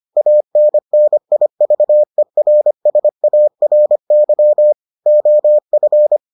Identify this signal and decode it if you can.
Morse code, heard as a keyed tone, about 25 wpm, 600 Hz. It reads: ANNIVERSARY OF